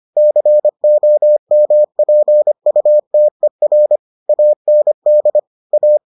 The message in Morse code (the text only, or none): COMPUTER AND A